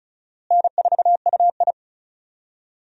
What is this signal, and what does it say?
Morse code, heard as a keyed tone, about 36 wpm, 700 Hz. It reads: N4UI
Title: USA Call Signs - 4 Characters - Repeated 2x - 35wpm